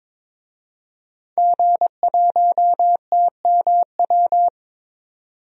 Morse code, heard as a keyed tone, about 22 wpm, 700 Hz. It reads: G1TMW
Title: DXCC-100 Call Signs - 5 Characters - 22wpm